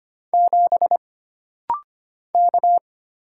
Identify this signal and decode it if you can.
Morse code, heard as a keyed tone, about 25 wpm, 700 Hz. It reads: 7 K
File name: Single Character (Letter:Number:Punctuation:Prosign) - Rapid-Fire - 25wpm